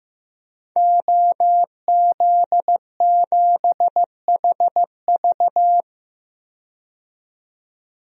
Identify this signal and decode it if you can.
Morse code, heard as a keyed tone, about 15 wpm, 700 Hz. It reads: OZ7HV
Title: DXCC-100 Call Signs - 5 Characters - 15wpm